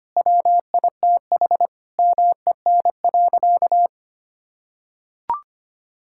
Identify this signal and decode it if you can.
Morse code, heard as a keyed tone, about 25 wpm, 700 Hz. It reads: WITH MEN.
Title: Sentences from Top 1000 Words - Review Part 5 - 25wpm